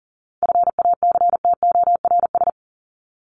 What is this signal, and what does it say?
Morse code, heard as a keyed tone, about 40 wpm, 700 Hz. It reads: FACTORS
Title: ES:S Suffix - Top 100 Words with Base Words 40wpm